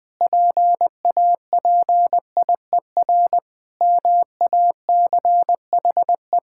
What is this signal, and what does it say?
Morse code, heard as a keyed tone, about 20 wpm, 700 Hz. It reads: PAPIER MACHE
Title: Pangram Challenge v2 - 20wpm